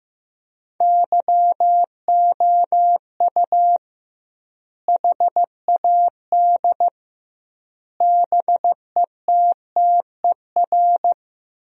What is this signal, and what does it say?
Morse code, heard as a keyed tone, about 15 wpm, 700 Hz. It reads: YOU HAD BETTER
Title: 2x Word-Spacing - Sentences from Top 400 Words - 15wpm